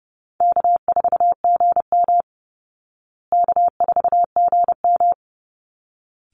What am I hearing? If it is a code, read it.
Morse code, heard as a keyed tone, about 30 wpm, 700 Hz. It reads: K4GM K4GM
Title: Easy Call Signs Repeated 2x - 30wpm